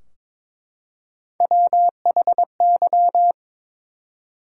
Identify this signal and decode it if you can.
Morse code, heard as a keyed tone, about 22 wpm, 700 Hz. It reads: WHY
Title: Three Letter Words - 22wpm